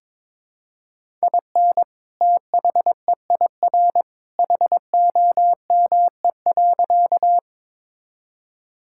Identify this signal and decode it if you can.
Morse code, heard as a keyed tone, about 22 wpm, 700 Hz. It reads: IN THEIR HOME.
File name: practice / 22wpm / Sentences with 2-Word Phrases - 22wpm